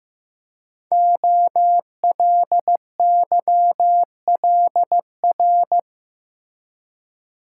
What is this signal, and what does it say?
Morse code, heard as a keyed tone, about 15 wpm, 700 Hz. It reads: OLYLR
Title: Code Groups - 5 Letter:Numbers - 15wpm